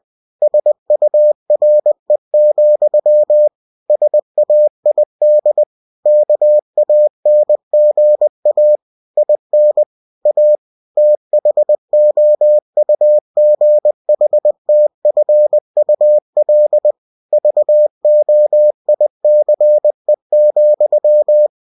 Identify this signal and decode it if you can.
Morse code, heard as a keyed tone, about 20 wpm, 600 Hz. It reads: SURE, SAID KANGA IN A THOUGHTFUL VOICE,
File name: cw-books / winnie-the-pooh / winnie-20wpm-page-0071